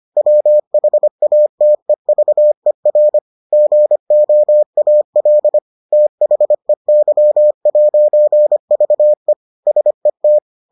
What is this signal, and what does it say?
Morse code, heard as a keyed tone, about 25 wpm, 600 Hz. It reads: WHATEVER GOAL THEY'VE SET